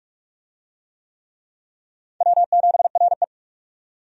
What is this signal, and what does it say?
Morse code, heard as a keyed tone, about 44 wpm, 700 Hz. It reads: W7RE
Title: USA Call Signs - 4 Characters - 45wpm